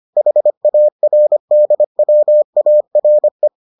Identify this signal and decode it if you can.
Morse code, heard as a keyed tone, about 25 wpm, 600 Hz. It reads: HARDWARE